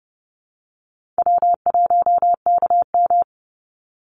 Morse code, heard as a keyed tone, about 30 wpm, 700 Hz. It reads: W1KM
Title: Easy Call Signs - 30wpm